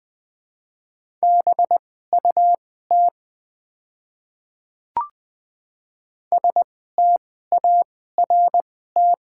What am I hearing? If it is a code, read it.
Morse code, heard as a keyed tone, about 20 wpm, spaced as for 15 wpm, 700 Hz. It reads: BUT START
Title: 2x Character Spacing - Top 500 Words - 20wpm